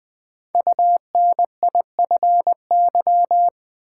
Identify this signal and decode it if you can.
Morse code, heard as a keyed tone, about 20 wpm, 700 Hz. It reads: UNIFY
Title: Build A Word - 1 Letter At A Time - 3 to 5-Letter Words - 20wpm